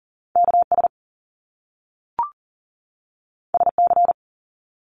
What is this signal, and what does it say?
Morse code, heard as a keyed tone, about 38 wpm, 700 Hz. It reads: KS SC
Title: US States 40wpm